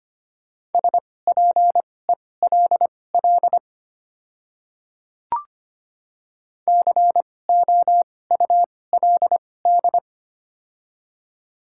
Morse code, heard as a keyed tone, about 25 wpm, spaced as for 18 wpm, 700 Hz. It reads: SPELL COULD
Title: 2x Character Spacing - Top 200 Words - 25wpm